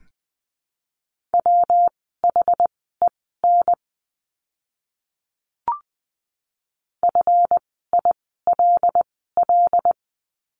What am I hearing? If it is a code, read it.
Morse code, heard as a keyed tone, about 20 wpm, spaced as for 15 wpm, 700 Hz. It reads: WHEN FILL
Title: 139 Lesson 23f - Letter f - Words 20wpm